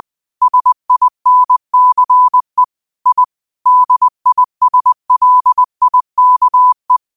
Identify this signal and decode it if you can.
Morse code, heard as a keyed tone, about 20 wpm, 1000 Hz. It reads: SINCE I DISLIKE